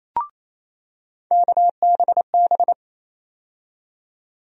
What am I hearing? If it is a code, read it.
Morse code, heard as a keyed tone, about 28 wpm, 700 Hz. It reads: KBB
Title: USA Call Sign Postfixes - 3 Letters 28wpm